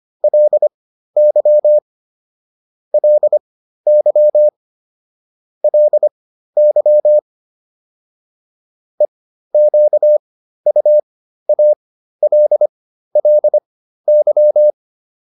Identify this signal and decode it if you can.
Morse code, heard as a keyed tone, about 25 wpm, spaced as for 13 wpm, 600 Hz. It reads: LY LY LY EQUALLY
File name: LY7 13